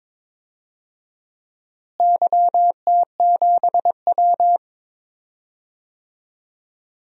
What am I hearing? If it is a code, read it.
Morse code, heard as a keyed tone, about 22 wpm, 700 Hz. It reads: YT7W
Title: DXCC-100 Call Signs - 4 Characters - Repeated 2x - 22wpm